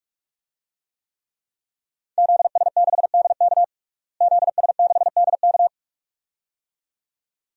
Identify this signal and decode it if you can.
Morse code, heard as a keyed tone, about 43 wpm, 700 Hz. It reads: ZS6DK ZS6DK
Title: DXCC-100 Call Signs - 5 Characters - Repeated 2x - 45wpm